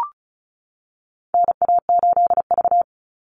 Morse code, heard as a keyed tone, about 34 wpm, 700 Hz. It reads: NA8V
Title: Call Signs - 35wpm